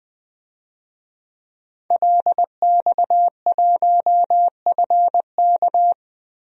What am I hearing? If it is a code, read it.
Morse code, heard as a keyed tone, about 20 wpm, 700 Hz. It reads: LX1FK
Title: DXCC-100 Call Signs - 5 Characters - Repeated 2x - 20wpm